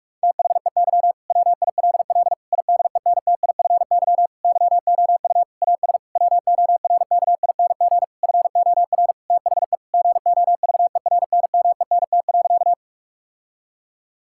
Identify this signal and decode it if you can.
Morse code, heard as a keyed tone, about 46 wpm, 700 Hz. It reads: THEY WILL IDENTIFY YOU AS WORKING FOR THE GOVERNMENT.
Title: Sentences from Top 1000 Words - Review Part 6 - 45wpm